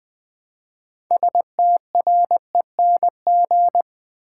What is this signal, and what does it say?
Morse code, heard as a keyed tone, about 21 wpm, 700 Hz. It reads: STRENG
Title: Build A Word - 3 Letters At A Time - 8-Letter Words - 20wpm